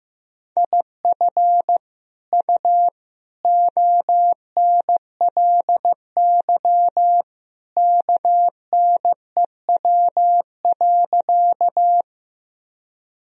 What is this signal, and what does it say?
Morse code, heard as a keyed tone, about 15 wpm, 700 Hz. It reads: IF U ONLY KNEW.
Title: Sentences from Top 400 Words and Abbreviations 15wpm